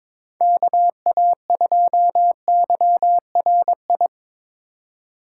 Morse code, heard as a keyed tone, about 22 wpm, 700 Hz. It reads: KA2YRI
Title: USA Call Signs - 2x Character Spacing - 22wpm